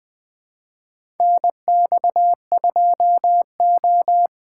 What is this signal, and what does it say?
Morse code, heard as a keyed tone, about 20 wpm, 700 Hz. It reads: NX2O
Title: USA Call Signs - 4 Characters - 2x Character Spacing - 20wpm